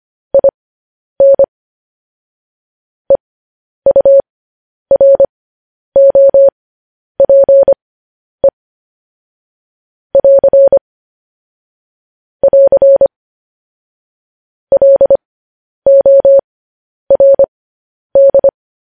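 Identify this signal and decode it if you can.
Morse code, heard as a keyed tone, about 25 wpm, spaced as for 10 wpm, 550 Hz. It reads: IN EUROPE <AR> <AR> LORD
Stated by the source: Morse code news headlines broadcast, opening with vvv.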